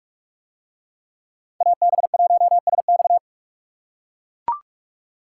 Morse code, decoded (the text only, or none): AB1SX